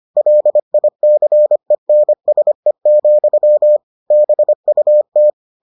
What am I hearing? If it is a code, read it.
Morse code, heard as a keyed tone, about 25 wpm, 600 Hz. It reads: LICENSE, BUT